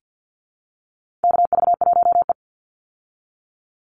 Morse code, heard as a keyed tone, about 50 wpm, 700 Hz. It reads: KV1E
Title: Call Signs Repeated 2x - 50wpm